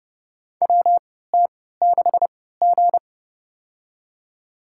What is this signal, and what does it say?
Morse code, heard as a keyed tone, about 30 wpm, spaced as for 17 wpm, 700 Hz. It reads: WT6G
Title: USA Call Signs - 4 Characters - 3x Character Spacing - 30wpm